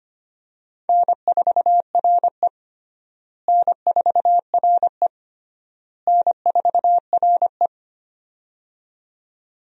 Morse code, heard as a keyed tone, about 25 wpm, 700 Hz. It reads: N4RE N4RE N4RE
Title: USA Call Signs - 4 Characters - Repeated 3x - 25wpm